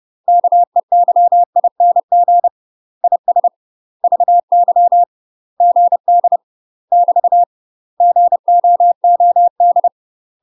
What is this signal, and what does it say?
Morse code, heard as a keyed tone, about 30 wpm, 700 Hz. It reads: KEYING IS VY GD <BT> GOOD
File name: UK MASTER QSOs TRAINING / 30WPM / UK MASTER QSOs 0039